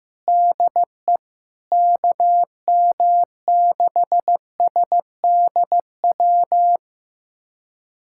Morse code, heard as a keyed tone, about 16 wpm, 700 Hz. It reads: DE KM6SDW
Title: USA Callsign DE Callsign - 6 Characters - 15wpm